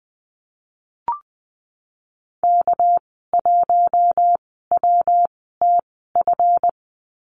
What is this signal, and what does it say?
Morse code, heard as a keyed tone, about 20 wpm, spaced as for 15 wpm, 700 Hz. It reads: K1WTF
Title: 219 Lesson 35g - Number 0 - Call Signs 20wpm